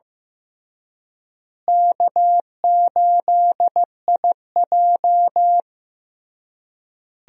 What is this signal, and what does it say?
Morse code, heard as a keyed tone, about 15 wpm, 700 Hz. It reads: K8IJ
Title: USA Call Signs - 4 Characters - Repeated 3x - 15wpm